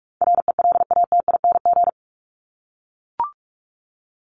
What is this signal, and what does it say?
Morse code, heard as a keyed tone, about 44 wpm, 700 Hz. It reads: RELATING
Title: ING Suffix - Top 100 Words with Base Words 45wpm